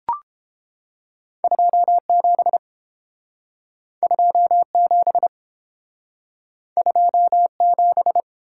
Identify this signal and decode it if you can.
Morse code, heard as a keyed tone, about 29 wpm, 700 Hz. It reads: 27 27 27